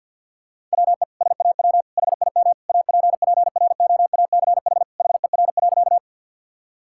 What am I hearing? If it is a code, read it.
Morse code, heard as a keyed tone, about 50 wpm, 700 Hz. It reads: WE SAW HIM APPROACH HER.
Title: Sentences from Top 1000 Words - Review Part 6 - 50wpm